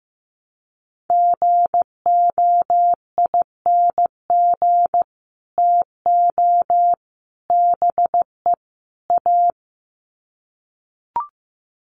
Morse code, heard as a keyed tone, about 15 wpm, 700 Hz. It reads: GOING TO BE A
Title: Sets of 4 Words 15wpm